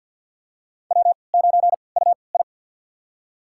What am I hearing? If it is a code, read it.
Morse code, heard as a keyed tone, about 47 wpm, spaced as for 28 wpm, 700 Hz. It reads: W9UI
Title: USA Call Signs - 4 Characters - 3x Character Spacing - 50wpm